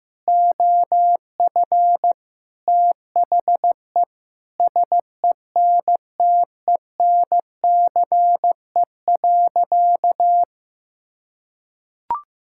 Morse code, heard as a keyed tone, about 15 wpm, 700 Hz. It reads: OF THE SENTENCE.
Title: Sentences from Top 1000 Words - Review Part 6 - 15wpm